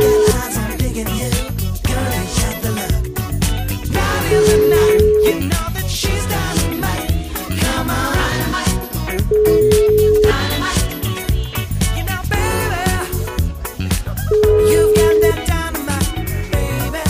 Ringback tone (Europe, 425 Hz), with noise.